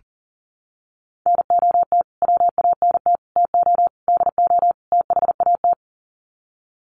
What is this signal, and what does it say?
Morse code, heard as a keyed tone, about 40 wpm, 700 Hz. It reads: NOT WANT TO DO THAT
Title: Sets of 5 Words Encore 40wpm